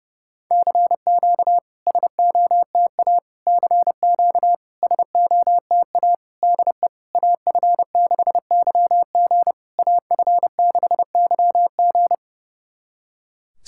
Morse code, heard as a keyed tone, about 30 wpm, 700 Hz. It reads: CQ SOTA CQ SOTA DE AF6YG AF6YG
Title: SOTA QSO Practice 30wpm